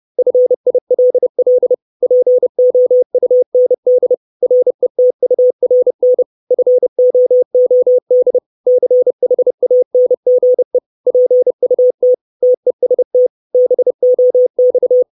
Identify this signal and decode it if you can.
Morse code, heard as a keyed tone, about 30 wpm, 500 Hz. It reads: FILL POUND RETURN FOOD CHANGE PUT TEST BOX
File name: 632 Words 30WPM